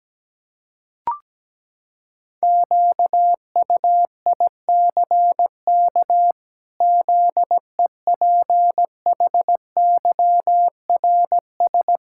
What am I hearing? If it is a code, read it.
Morse code, heard as a keyed tone, about 17 wpm, 700 Hz. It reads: QUICK ZEPHYRS
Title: Pangram Challenge v2 - 17wpm